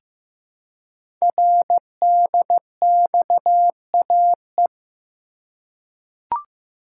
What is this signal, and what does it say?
Morse code, heard as a keyed tone, about 15 wpm, 700 Hz. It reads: RDXAE
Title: Code Groups - 5 Letters - 15wpm